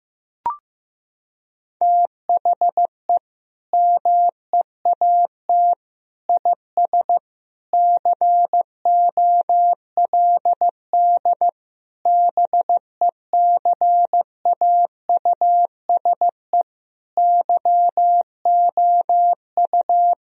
Morse code, heard as a keyed tone, about 16 wpm, 700 Hz. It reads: THE MEAT IS COLD BECAUSE YOU
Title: Sentences from Top 1000 Words - Review Part 3 - 15wpm